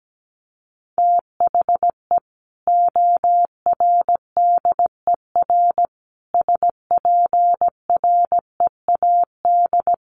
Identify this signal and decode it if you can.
Morse code, heard as a keyed tone, about 17 wpm, 700 Hz. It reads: THE ORDER SPREAD
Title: Sentences from Top 1200 Words 17wpm